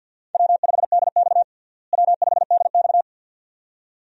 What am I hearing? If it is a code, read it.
Morse code, heard as a keyed tone, about 52 wpm, 700 Hz. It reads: W5DX W5DX
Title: USA Call Signs - 4 Characters - Repeated 2x - 50wpm